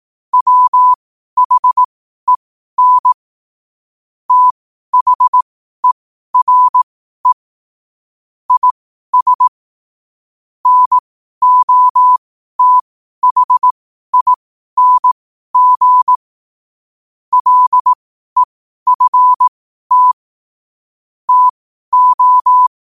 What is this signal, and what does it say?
Morse code, heard as a keyed tone, about 18 wpm, spaced as for 12 wpm, 1000 Hz. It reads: WHEN THERE IS NOTHING LEFT TO
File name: qotdfiles / 12wpm